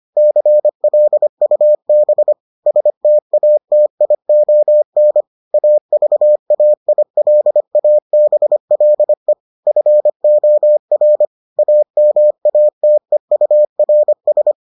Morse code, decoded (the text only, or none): CLUB STATION AVAILABLE FOR AMATEURS